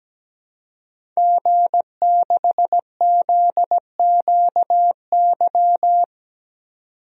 Morse code, heard as a keyed tone, about 17 wpm, 700 Hz. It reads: G6ZQY
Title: DXCC-100 Call Signs - 5 Characters - 17wpm